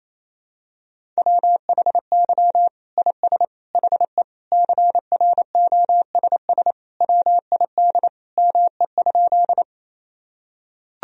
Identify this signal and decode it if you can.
Morse code, heard as a keyed tone, about 28 wpm, 700 Hz. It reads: WHY IS HE CROSS WID ME?
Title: Sentences from Top 300 Words and Abbreviations 28wpm